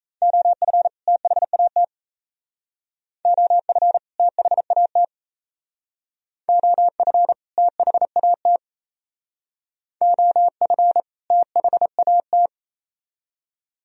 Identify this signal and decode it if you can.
Morse code, heard as a keyed tone, about 35 wpm, 700 Hz. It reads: OF THAT OF THAT OF THAT OF THAT
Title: Speed-Racing - Sets of 2 Words 28wpm